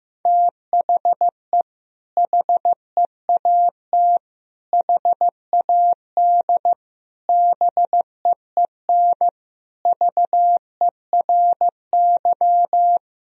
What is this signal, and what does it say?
Morse code, heard as a keyed tone, about 15 wpm, 700 Hz. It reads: THE HEAT HAD BEEN VERY
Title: Sentences from Top 1000 Words - Review Part 3 - 15wpm